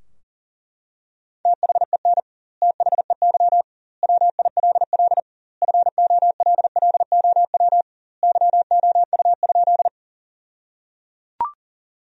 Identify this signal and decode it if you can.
Morse code, heard as a keyed tone, about 37 wpm, 700 Hz. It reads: THEN THEY WILL FOLLOW YOU?
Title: 2x Word-Spacing - Sentences from Top 200 Words - 40wpm